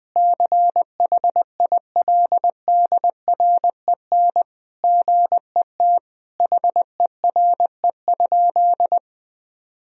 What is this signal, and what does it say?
Morse code, heard as a keyed tone, about 20 wpm, 700 Hz. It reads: CHILDREN GET HERE?
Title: Sentences from Top 1000 Words - Review Part 1 - 20wpm